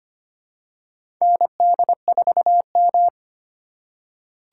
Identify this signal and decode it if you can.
Morse code, heard as a keyed tone, about 25 wpm, 700 Hz. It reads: ND4M
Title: USA Call Signs - 4 Characters - 8x Character Spacing - 25wpm